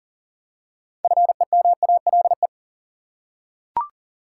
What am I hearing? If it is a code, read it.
Morse code, heard as a keyed tone, about 40 wpm, 700 Hz. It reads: FEMALE
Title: Six Letter Words - 40wpm